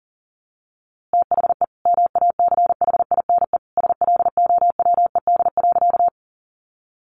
Morse code, heard as a keyed tone, about 40 wpm, 700 Hz. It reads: THE MACHINE SLOWED.
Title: A Sound Of Thunder 40wpm